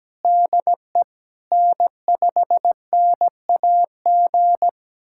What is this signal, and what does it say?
Morse code, heard as a keyed tone, about 17 wpm, 700 Hz. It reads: DE N5NAG
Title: USA Callsign DE Callsign 17wpm